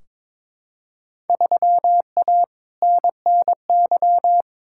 Morse code, heard as a keyed tone, about 22 wpm, 700 Hz. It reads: 3A NNY
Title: Field Day Exchanges - 22wpm